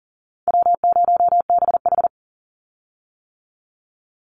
Morse code, heard as a keyed tone, about 40 wpm, 700 Hz. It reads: W0BH
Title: Easy Call Signs - 40wpm